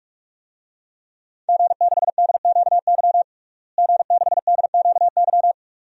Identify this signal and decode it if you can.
Morse code, heard as a keyed tone, about 46 wpm, 700 Hz. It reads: G6DQY G6DQY
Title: DXCC-100 Call Signs - 5 Characters - Repeated 2x - 45wpm